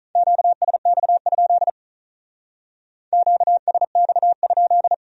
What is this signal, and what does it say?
Morse code, heard as a keyed tone, about 44 wpm, 700 Hz. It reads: QSX? QSX?
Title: Speed-Racing Q-Codes - 35wpm